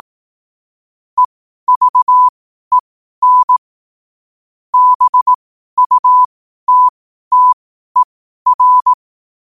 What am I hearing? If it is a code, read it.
Morse code, heard as a keyed tone, about 18 wpm, spaced as for 12 wpm, 1000 Hz. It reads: EVEN BUTTER
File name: qotdfiles / 12wpm